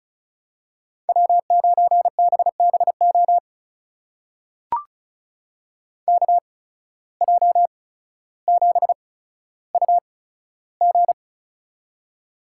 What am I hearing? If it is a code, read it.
Morse code, heard as a keyed tone, about 36 wpm, spaced as for 15 wpm, 700 Hz. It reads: W9BBO KJ7UG
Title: USA Call Signs - 5 Characters - 8x Character Spacing - 35wpm